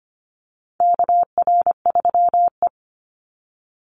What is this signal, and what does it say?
Morse code, heard as a keyed tone, about 25 wpm, 700 Hz. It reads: KR3E
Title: Call Signs - 25wpm